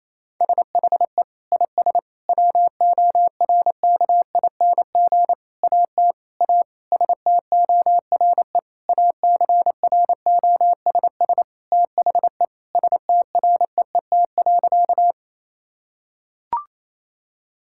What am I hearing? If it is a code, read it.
Morse code, heard as a keyed tone, about 28 wpm, 700 Hz. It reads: SHE IS WORKING AT A STORE ACROSS THE STREET.
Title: Sentences from Top 1000 Words - Review Part 5 - 28wpm